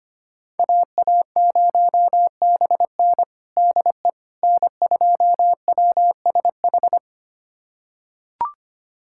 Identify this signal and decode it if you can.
Morse code, heard as a keyed tone, about 25 wpm, 700 Hz. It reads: AA0BN DE N2WSH
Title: USA Callsign DE Callsign - 5 Characters - 25wpm